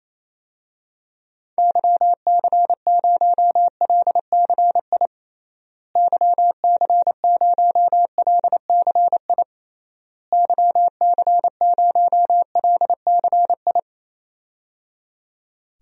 Morse code, heard as a keyed tone, about 28 wpm, 700 Hz. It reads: YC0LCI YC0LCI YC0LCI
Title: DXCC-100 Call Signs - 6 Characters - Repeated 3x - 28wpm